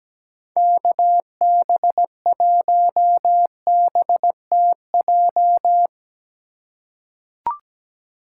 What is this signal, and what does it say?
Morse code, heard as a keyed tone, about 17 wpm, 700 Hz. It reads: KB1BTJ
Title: USA Call Signs - 6 Characters - 2x Character Spacing - 17wpm